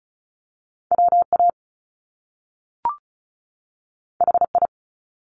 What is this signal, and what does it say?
Morse code, heard as a keyed tone, about 36 wpm, 700 Hz. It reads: WA HI
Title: US States 35wpm